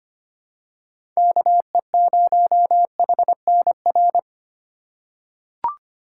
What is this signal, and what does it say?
Morse code, heard as a keyed tone, about 25 wpm, 700 Hz. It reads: KE0HNR
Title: USA Call Signs - 2x Character Spacing - 25wpm